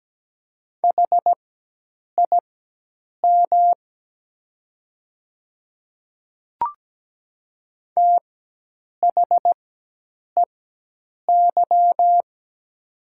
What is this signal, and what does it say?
Morse code, heard as a keyed tone, about 17 wpm, spaced as for 8 wpm, 700 Hz. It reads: HIM THEY
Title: 4x Character Spacing - Top 100 Words - 17wpm